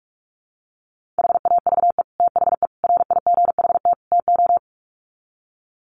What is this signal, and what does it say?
Morse code, heard as a keyed tone, about 44 wpm, 700 Hz. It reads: HAVE THE RIGHT TO
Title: Sets of 4 Words 45wpm